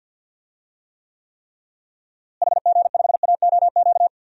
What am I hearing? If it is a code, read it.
Morse code, heard as a keyed tone, about 50 wpm, 700 Hz. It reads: HG5AOQ
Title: DXCC-100 Call Signs - 6 Characters - Repeated 3x - 50wpm